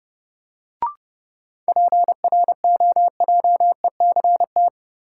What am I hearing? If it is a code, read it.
Morse code, heard as a keyed tone, about 32 wpm, 700 Hz. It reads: PROJECT
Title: Sentences from Top 1400 Words - New Word Drill Alt - 30wpm